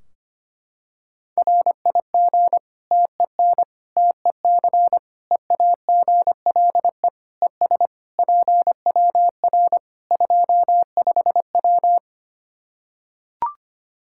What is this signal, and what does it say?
Morse code, heard as a keyed tone, about 25 wpm, 700 Hz. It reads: RIG TEN TEC EAGLE ES PWR 25W
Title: Practice QSO - Exchange 2 - Info - 25wpm